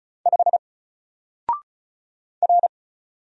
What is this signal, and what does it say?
Morse code, heard as a keyed tone, about 35 wpm, 700 Hz. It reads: 5 R
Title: Single Letter-Number Alt 35wpm